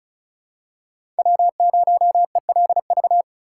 Morse code, heard as a keyed tone, about 36 wpm, 700 Hz. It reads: W0ELV
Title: Code Groups - 5 Letter:Numbers - 35wpm